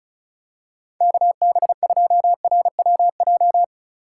Morse code, heard as a keyed tone, about 34 wpm, 700 Hz. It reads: KB2RWJ
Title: USA Call Signs - 6 Characters - 4x Character Spacing - 35wpm